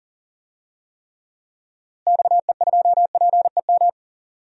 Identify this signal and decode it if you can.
Morse code, heard as a keyed tone, about 40 wpm, 700 Hz. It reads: XE2PEM
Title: DXCC-100 Call Signs - 6 Characters - Repeated 3x - 40wpm